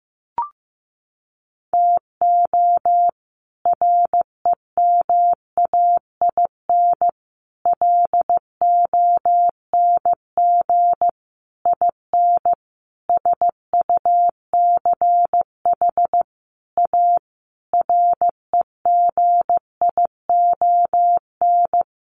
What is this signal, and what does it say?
Morse code, heard as a keyed tone, about 15 wpm, 700 Hz. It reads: TO REMAIN LONG IN SUCH A REGION
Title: Sentences from Top 1500 Words 15wpm